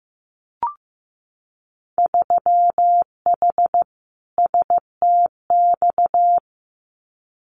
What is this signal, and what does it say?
Morse code, heard as a keyed tone, about 15 wpm, 700 Hz. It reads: 3H STX
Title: Winter Field Day Exchanges 15wpm